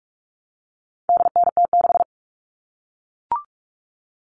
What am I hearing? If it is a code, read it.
Morse code, heard as a keyed tone, about 44 wpm, 700 Hz. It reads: DNT6
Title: All Cut Numbers - 4-Digit 45wpm